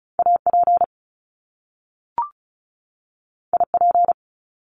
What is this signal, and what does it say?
Morse code, heard as a keyed tone, about 36 wpm, 700 Hz. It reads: AP IP
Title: Top Two Letter Combinations in Words 35wpm